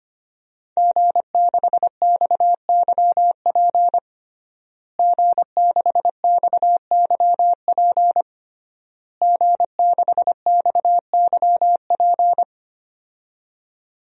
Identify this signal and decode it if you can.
Morse code, heard as a keyed tone, about 25 wpm, 700 Hz. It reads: G6XYP G6XYP G6XYP
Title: DXCC-100 Call Signs - 5 Characters - Repeated 3x - 25wpm